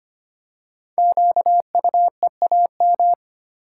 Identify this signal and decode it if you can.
Morse code, heard as a keyed tone, about 25 wpm, 700 Hz. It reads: QUEAM
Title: Code Groups - 5 Letters - 25wpm